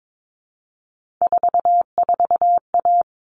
Morse code, heard as a keyed tone, about 22 wpm, 700 Hz. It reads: 44A
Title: All Cut Numbers - 3-Digit 22wpm